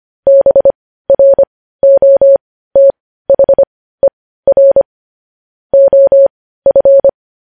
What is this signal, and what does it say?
Morse code, heard as a keyed tone, about 25 wpm, spaced as for 15 wpm, 550 Hz. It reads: BROTHER OF
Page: News Headlines in Morse Code at 15 WPM